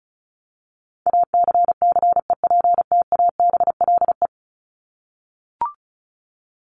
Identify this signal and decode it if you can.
Morse code, heard as a keyed tone, about 34 wpm, 700 Hz. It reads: ACCEPTABLE
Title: Top 1501-1750 Words 35wpm